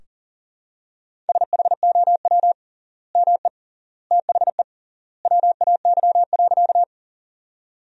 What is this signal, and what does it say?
Morse code, heard as a keyed tone, about 40 wpm, 700 Hz. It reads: SHOW ME THE WAY.
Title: 3x Word-Spacing - Sentences from Top 200 Words - 40wpm